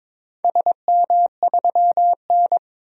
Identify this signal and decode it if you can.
Morse code, heard as a keyed tone, about 21 wpm, 700 Hz. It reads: SM3N
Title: DXCC-100 Call Signs - 4 Characters - 22wpm